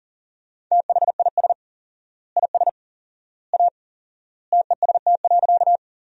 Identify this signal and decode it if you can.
Morse code, heard as a keyed tone, about 40 wpm, 700 Hz. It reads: THIS IS A TEST.
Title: 4x Word-Spacing - Sentences from Top 500 Words - 40wpm